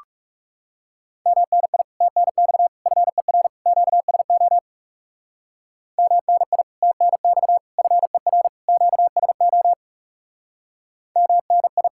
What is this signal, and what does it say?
Morse code, heard as a keyed tone, about 37 wpm, 700 Hz. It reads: MNI TNX FER QSO MNI TNX FER QSO MNI
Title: Speed-Racing Common QSO Phrases Encore - 30wpm